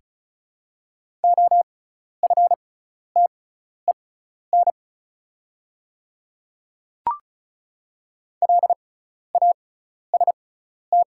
Morse code, heard as a keyed tone, about 34 wpm, spaced as for 12 wpm, 700 Hz. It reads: OFTEN LAST
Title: 6x Character Spacing - Top 300 Words - 35wpm